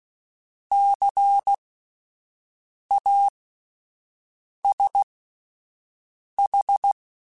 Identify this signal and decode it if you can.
Morse code, heard as a keyed tone, about 16 wpm, spaced as for 5 wpm, 800 Hz. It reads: CASH
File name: Alphabetical / Alphabetical mp3s / Cs @5